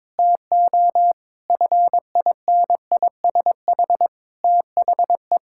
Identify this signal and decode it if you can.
Morse code, heard as a keyed tone, about 22 wpm, 700 Hz. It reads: TO FINISH THE
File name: Sentences with 7-Word Phrases - 22wpm